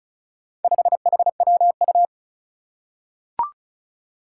Morse code, heard as a keyed tone, about 36 wpm, 700 Hz. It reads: 5HWU
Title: Four Challenging Letter-Number Combinations 35wpm